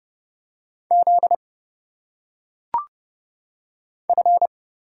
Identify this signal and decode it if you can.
Morse code, heard as a keyed tone, about 29 wpm, 700 Hz. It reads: Z F
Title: Single Letters - Phonetic Pronunciation 30wpm